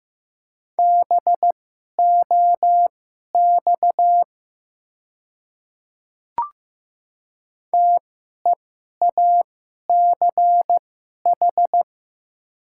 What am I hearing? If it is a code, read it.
Morse code, heard as a keyed tone, about 15 wpm, spaced as for 11 wpm, 700 Hz. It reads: BOX TEACH
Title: 2x Character Spacing - Top 500 Words - 15wpm